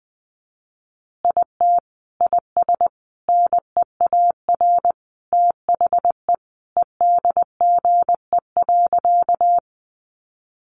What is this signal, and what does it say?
Morse code, heard as a keyed tone, about 20 wpm, 700 Hz. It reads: IT IS NEAR THE EDGE.
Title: Additional Easy Phrases 20wpm